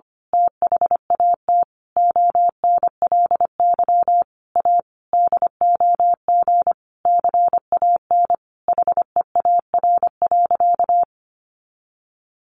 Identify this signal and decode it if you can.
Morse code, heard as a keyed tone, about 25 wpm, 700 Hz. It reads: THAT ONLY A DOG CAN HEAR.
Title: A Sound Of Thunder 25wpm